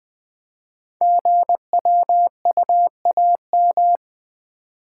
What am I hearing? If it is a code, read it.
Morse code, heard as a keyed tone, about 20 wpm, 700 Hz. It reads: GWUAM
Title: Code Groups - 5 Letters - 20wpm